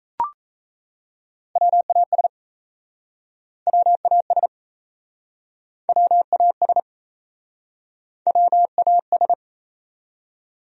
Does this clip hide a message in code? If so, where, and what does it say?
Yes: Morse code from 1.6 to 9.3 s, heard as a keyed tone, about 40 wpm, 700 Hz. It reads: WAS WAS WAS WAS